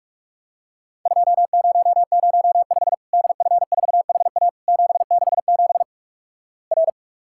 Morse code, heard as a keyed tone, about 46 wpm, 700 Hz. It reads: 2005 DF4HA 767 R
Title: Worked All Europe DX Contest - QTC Exchange - 45wpm